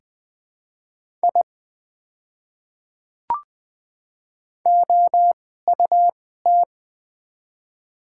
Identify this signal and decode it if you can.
Morse code, heard as a keyed tone, about 20 wpm, spaced as for 14 wpm, 700 Hz. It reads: I OUT
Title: 2x Character Spacing - Top 200 Words - 20wpm